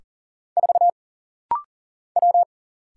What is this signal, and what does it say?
Morse code, heard as a keyed tone, about 40 wpm, 700 Hz. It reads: 4 W
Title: Single Letter-Number - Rapid-Fire 40wpm